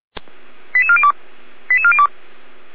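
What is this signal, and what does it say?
Five-tone selective call (ZVEI), heard as quick runs of five tones, tones of about 70 ms. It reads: 88402 88402
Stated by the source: ZVEI five-tone selective call received by SDR at Feuerwehr Orlamünde.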